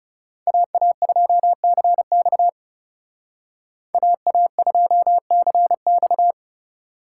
Morse code, heard as a keyed tone, about 38 wpm, 700 Hz. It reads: AA2CX AA2CX